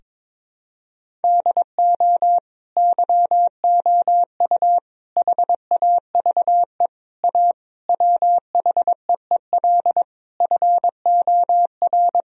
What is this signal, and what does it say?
Morse code, heard as a keyed tone, about 21 wpm, 700 Hz. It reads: DO YOU HAVE A WHEEL FOR
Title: Sentences from Top 1000 Words - Review Part 3 - 22wpm